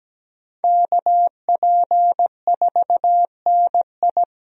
Morse code, heard as a keyed tone, about 17 wpm, 700 Hz. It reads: KP4NI
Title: USA Call Signs - 5 Characters - Repeated 3x - 17wpm